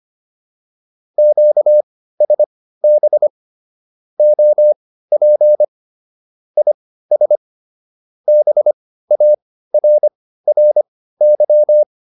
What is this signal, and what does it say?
Morse code, heard as a keyed tone, about 25 wpm, spaced as for 15 wpm, 600 Hz. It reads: QSB OP IS BARRY